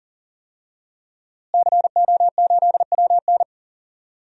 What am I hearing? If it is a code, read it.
Morse code, heard as a keyed tone, about 40 wpm, 700 Hz. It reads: CO8WN
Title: DXCC-100 Call Signs - 5 Characters - Repeated 2x - 40wpm